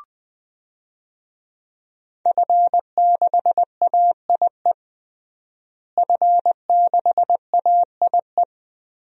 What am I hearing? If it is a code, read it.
Morse code, heard as a keyed tone, about 20 wpm, 700 Hz. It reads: F6AIE F6AIE
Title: DXCC-100 Call Signs - 5 Characters - Repeated 3x - 20wpm